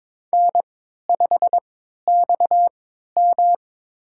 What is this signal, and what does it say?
Morse code, heard as a keyed tone, about 22 wpm, spaced as for 13 wpm, 700 Hz. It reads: N5XM
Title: USA Call Signs - 3x Character Spacing - 22wpm